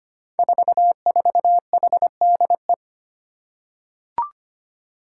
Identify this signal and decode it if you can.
Morse code, heard as a keyed tone, about 25 wpm, 700 Hz. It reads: 44HDE
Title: Five Challenging Letter-Number Combinations 25wpm